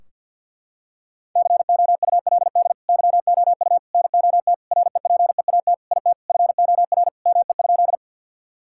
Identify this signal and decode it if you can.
Morse code, heard as a keyed tone, about 50 wpm, 700 Hz. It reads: COULD YOU NOT REPEAT IT FOR ME?
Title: Sentences from Top 1600 Words 50wpm